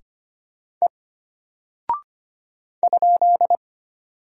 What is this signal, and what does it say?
Morse code, heard as a keyed tone, about 25 wpm, 700 Hz. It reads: E ?
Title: Single Character (Letter:Number:Punctuation:Prosign) Alt 25wpm